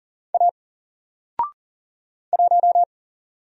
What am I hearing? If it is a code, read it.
Morse code, heard as a keyed tone, about 40 wpm, 700 Hz. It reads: A 1
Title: Single Letter-Number Alt 40wpm